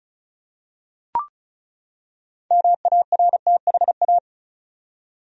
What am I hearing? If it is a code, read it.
Morse code, heard as a keyed tone, about 36 wpm, 700 Hz. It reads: MARTHA
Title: USA Common Names - 35wpm